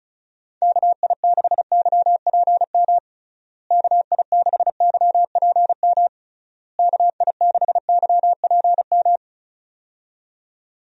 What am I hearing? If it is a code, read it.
Morse code, heard as a keyed tone, about 36 wpm, 700 Hz. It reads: KI6YPM KI6YPM KI6YPM
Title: USA Call Signs - 6 Characters - Repeated 3x - 35wpm